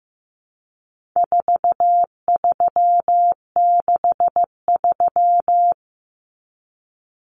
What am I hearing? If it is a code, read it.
Morse code, heard as a keyed tone, about 15 wpm, 700 Hz. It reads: 4363
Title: Common Cut Numbers - 4-Digit 15wpm